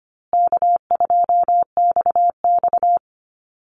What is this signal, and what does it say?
Morse code, heard as a keyed tone, about 25 wpm, 700 Hz. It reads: K2XX
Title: Call Signs Repeated 2x - 25wpm